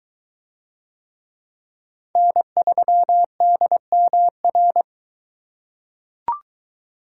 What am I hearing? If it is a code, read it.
Morse code, heard as a keyed tone, about 21 wpm, 700 Hz. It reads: N3DMR